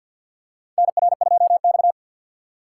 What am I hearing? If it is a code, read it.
Morse code, heard as a keyed tone, about 50 wpm, 700 Hz. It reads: NR2X
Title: USA Call Signs - 4 Characters - 2x Character Spacing - 50wpm